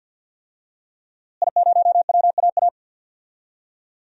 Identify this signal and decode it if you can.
Morse code, heard as a keyed tone, about 46 wpm, 700 Hz. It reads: I0WAA
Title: DXCC-100 Call Signs - 5 Characters - Repeated 2x - 50wpm